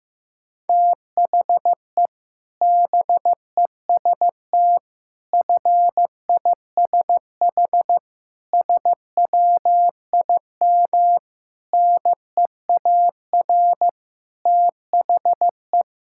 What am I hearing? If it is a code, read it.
Morse code, heard as a keyed tone, about 15 wpm, 700 Hz. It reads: THE BEST FISH SWIM NEAR THE
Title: Sentences from Top 1000 Words - Review Part 6 - 15wpm